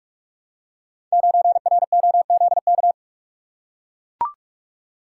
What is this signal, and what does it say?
Morse code, heard as a keyed tone, about 44 wpm, 700 Hz. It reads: 9ROZK
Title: Code Groups - 5 Letter:Numbers - 45wpm